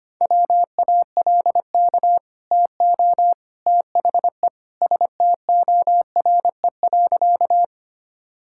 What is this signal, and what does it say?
Morse code, heard as a keyed tone, about 25 wpm, 700 Hz. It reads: WALK TO THE STORE.
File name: Easy Sentences with Hints - 25wpm